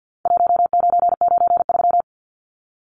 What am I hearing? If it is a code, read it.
Morse code, heard as a keyed tone, about 49 wpm, 700 Hz. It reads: 1993
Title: Numbers - Contest Style - 4-Digit 50wpm